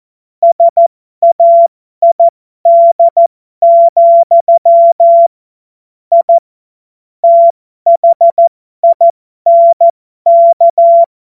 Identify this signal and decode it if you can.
Morse code, heard as a keyed tone, about 14 wpm, spaced as for 12 wpm, 680 Hz. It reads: SAID, I THINK
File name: Morsekurs / Witze